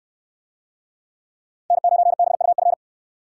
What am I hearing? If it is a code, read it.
Morse code, heard as a keyed tone, about 69 wpm, 700 Hz. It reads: N9BUF